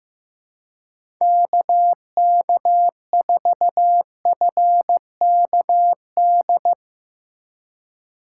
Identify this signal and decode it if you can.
Morse code, heard as a keyed tone, about 15 wpm, 700 Hz. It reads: KK4FKD
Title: USA Call Signs - 2x Character Spacing - 15wpm